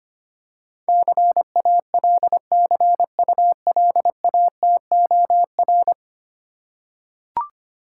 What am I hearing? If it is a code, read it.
Morse code, heard as a keyed tone, about 25 wpm, 690 Hz. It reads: CALCULATOR
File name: practice / 25wpm / Easy Words with Hints - 25wpm